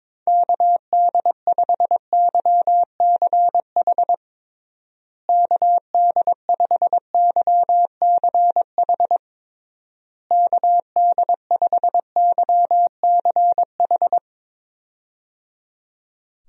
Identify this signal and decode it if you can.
Morse code, heard as a keyed tone, about 22 wpm, 700 Hz. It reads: KD5YCH KD5YCH KD5YCH
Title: USA Call Signs - 6 Characters - Repeated 3x - 22wpm